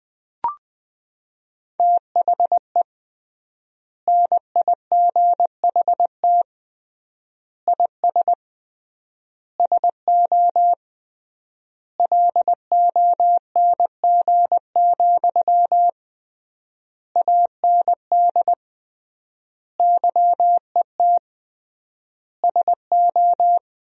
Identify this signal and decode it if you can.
Morse code, heard as a keyed tone, about 21 wpm, 700 Hz. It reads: THE NIGHT IS SO LONG, AND YET SO
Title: 3x Word-Spacing - Sentences from Top 500 Words - 20wpm